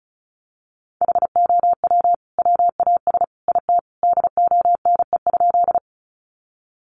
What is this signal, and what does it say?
Morse code, heard as a keyed tone, about 34 wpm, 700 Hz. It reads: HOW WAS IT DONE?
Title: Sentences from Top 500 Words 35wpm